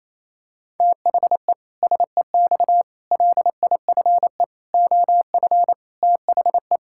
Morse code, heard as a keyed tone, about 28 wpm, 700 Hz. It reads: THE SEX LIFE OF THE
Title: Pangram Challenge v2 - 28wpm